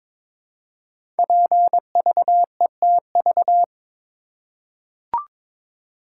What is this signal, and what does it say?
Morse code, heard as a keyed tone, about 23 wpm, 700 Hz. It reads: PVETV
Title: Code Groups - 5 Letters - 22wpm